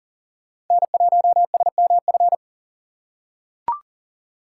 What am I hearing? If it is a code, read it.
Morse code, heard as a keyed tone, about 40 wpm, 700 Hz. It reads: N1SMF
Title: USA Call Signs - 5 Characters - 6x Character Spacing - 40wpm